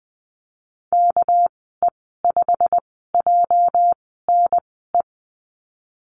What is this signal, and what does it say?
Morse code, heard as a keyed tone, about 20 wpm, spaced as for 15 wpm, 700 Hz. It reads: KE5JNE
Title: 245 Lesson 39g - Letter z - Call Signs 20wpm